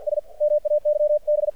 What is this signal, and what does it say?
Morse code, heard as a keyed tone, about 53 wpm, 600 Hz. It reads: S MAYB
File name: archived / CRSnet